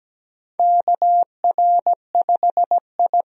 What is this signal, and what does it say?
Morse code, heard as a keyed tone, about 17 wpm, 700 Hz. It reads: KR5I
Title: USA Call Signs - 4 Characters - 17wpm